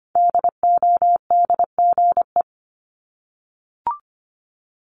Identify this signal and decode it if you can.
Morse code, heard as a keyed tone, about 25 wpm, 700 Hz. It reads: DODGE
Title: Multi-letter Phonemes Repeated 3x 25wpm